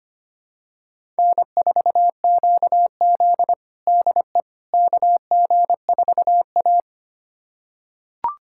Morse code, heard as a keyed tone, about 25 wpm, 700 Hz. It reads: N4QZ DE KG4A
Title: USA Callsign DE Callsign - 4 Characters - 25wpm